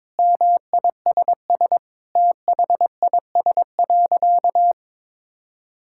Morse code, heard as a keyed tone, about 22 wpm, 700 Hz. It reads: MISS THIS.
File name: practice / 22wpm / Sentences with 4-Word Phrases - 22wpm